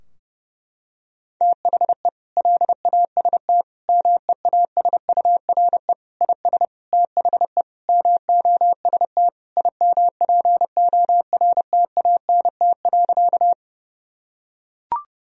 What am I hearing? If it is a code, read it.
Morse code, heard as a keyed tone, about 30 wpm, 700 Hz. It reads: THE LAST MEASURE IS THE MOST IMPORTANT.
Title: Sentences from Top 1000 Words - Review Part 2 - 30wpm